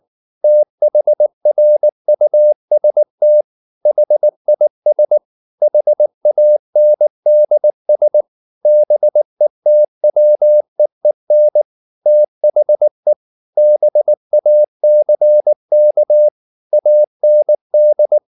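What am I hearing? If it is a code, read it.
Morse code, heard as a keyed tone, about 19 wpm, 600 Hz. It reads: THRUST HIS HANDS BETWEEN THE BACK AND
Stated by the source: Morse code at 19 wpm from CW med Gnister, reading Ken Follett's novel Never in English.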